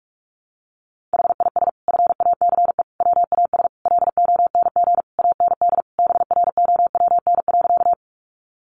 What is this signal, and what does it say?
Morse code, heard as a keyed tone, about 44 wpm, 700 Hz. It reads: HIS FACE WAS LONG AND BROWN.
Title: Sentences from Top 1200 Words 45wpm